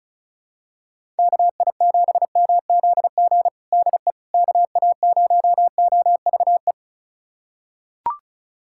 Morse code, heard as a keyed tone, about 34 wpm, 700 Hz. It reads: KI7MZG DE KA0OVE
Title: USA Callsign DE Callsign 35wpm